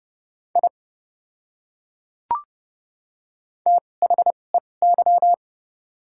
Morse code, heard as a keyed tone, about 29 wpm, spaced as for 21 wpm, 700 Hz. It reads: I THEY
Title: 2x Character Spacing - Top 200 Words - 30wpm